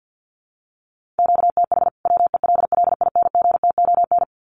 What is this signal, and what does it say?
Morse code, heard as a keyed tone, about 50 wpm, 700 Hz. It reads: QTH WELLINGTON
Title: Common QSO Phrases 50wpm